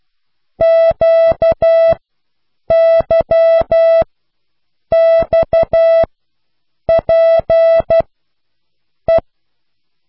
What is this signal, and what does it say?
Morse code, heard as a keyed tone, about 12 wpm, spaced as for 7 wpm, 660 Hz. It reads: QYXPE